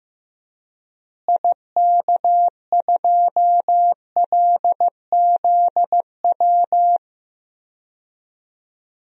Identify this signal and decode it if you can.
Morse code, heard as a keyed tone, about 15 wpm, 700 Hz. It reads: IK2LZW
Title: DXCC-100 Call Signs - 6 Characters - Repeated 2x - 15wpm